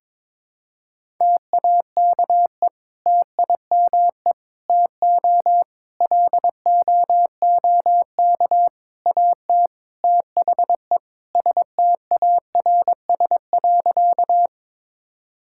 Morse code, heard as a keyed tone, about 22 wpm, 700 Hz. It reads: TAKE TIME TO LOOK AT THE STARS.
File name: practice / 22wpm / Sentences with 3-Word Phrases - 22wpm